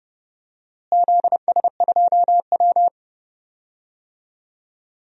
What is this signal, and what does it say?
Morse code, heard as a keyed tone, about 32 wpm, 700 Hz. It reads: ZS2W
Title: DXCC-100 Call Signs - 4 Characters - Repeated 3x - 30wpm